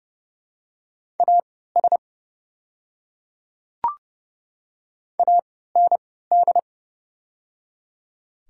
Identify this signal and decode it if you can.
Morse code, heard as a keyed tone, about 30 wpm, spaced as for 17 wpm, 700 Hz. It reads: AS AND
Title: 3x Character Spacing - Top 100 Words - 30wpm